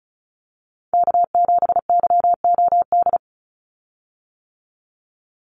Morse code, heard as a keyed tone, about 34 wpm, 700 Hz. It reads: K7YOD
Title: USA Call Signs 35wpm